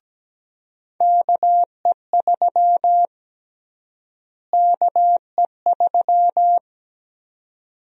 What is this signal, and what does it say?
Morse code, heard as a keyed tone, about 17 wpm, 700 Hz. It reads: KE3 KE3
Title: USA Call Sign Prefixes - Repeated 2x - 17wpm